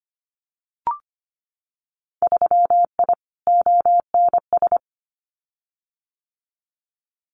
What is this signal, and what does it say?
Morse code, heard as a keyed tone, about 25 wpm, 700 Hz. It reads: 3I ONS
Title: Winter Field Day Exchanges Encore 25wpm